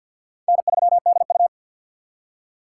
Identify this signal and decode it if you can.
Morse code, heard as a keyed tone, about 52 wpm, 700 Hz. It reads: N 3 D U
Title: USA Call Signs - 6x Character Spacing - 50wpm